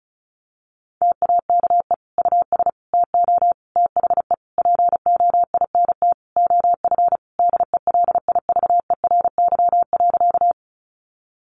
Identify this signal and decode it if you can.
Morse code, heard as a keyed tone, about 34 wpm, 700 Hz. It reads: TAKE US TO THE POINT OF DELIVERY.
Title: Sentences from Top 1500 Words 35wpm